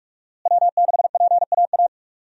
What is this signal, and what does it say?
Morse code, heard as a keyed tone, about 45 wpm, 700 Hz. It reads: W6PAA
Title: USA Call Signs - 5 Characters - Repeated 2x - 45wpm